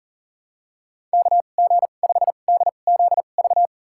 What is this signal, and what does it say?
Morse code, heard as a keyed tone, about 40 wpm, spaced as for 29 wpm, 700 Hz. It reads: KG5DZV
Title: USA Call Signs - 6 Characters - 2x Character Spacing - 40wpm